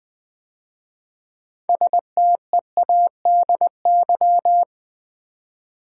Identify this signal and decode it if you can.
Morse code, heard as a keyed tone, about 20 wpm, 700 Hz. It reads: STEADY
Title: 6 Letter Words - First 3 Letters Given - 20wpm